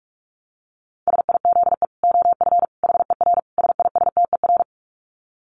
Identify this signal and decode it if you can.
Morse code, heard as a keyed tone, about 44 wpm, 700 Hz. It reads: SIZE OF HER SISTER
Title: Dit Challenge 45wpm